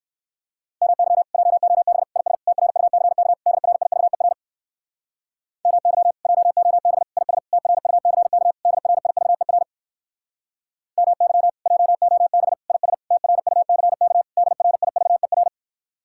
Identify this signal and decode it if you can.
Morse code, heard as a keyed tone, about 67 wpm, 700 Hz. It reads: MY JOB IS TRUCK DRIVER MY JOB IS TRUCK DRIVER MY JOB IS TRUCK DRIVER